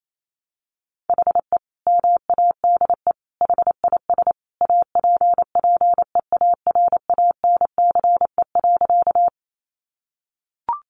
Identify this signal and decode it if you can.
Morse code, heard as a keyed tone, about 28 wpm, 700 Hz. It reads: HE MADE HIS APPEARANCE.
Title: Sentences from Top 1100 Words 28wpm